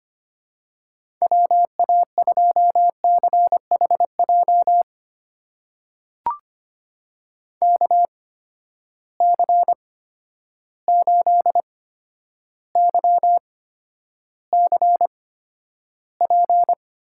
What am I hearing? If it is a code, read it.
Morse code, heard as a keyed tone, about 25 wpm, spaced as for 11 wpm, 700 Hz. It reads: WA2CHJ KC8YCP
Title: USA Call Signs - 6 Characters - 8x Character Spacing - 25wpm